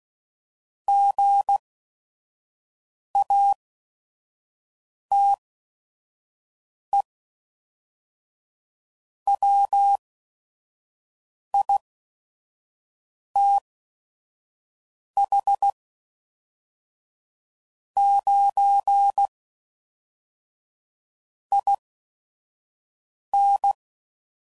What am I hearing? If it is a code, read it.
Morse code, heard as a keyed tone, about 16 wpm, spaced as for 5 wpm, 800 Hz. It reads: GATE WITH 9 IN
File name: CW Code Groups @5WPM / Technical Subjects / AND Gate